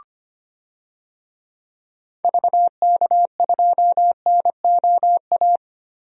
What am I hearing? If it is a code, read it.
Morse code, heard as a keyed tone, about 25 wpm, 700 Hz. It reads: VK2NOA
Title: DXCC-100 Call Signs - 6 Characters - Repeated 2x - 25wpm